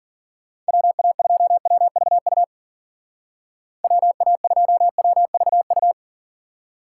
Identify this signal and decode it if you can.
Morse code, heard as a keyed tone, about 51 wpm, 700 Hz. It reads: WA2WVU WA2WVU